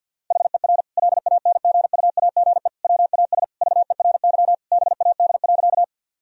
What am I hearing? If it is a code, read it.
Morse code, heard as a keyed tone, about 52 wpm, 700 Hz. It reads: HER LANGUAGE WAS VERY BAD.
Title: Sentences from Top 1000 Words - Review Part 3 - 50wpm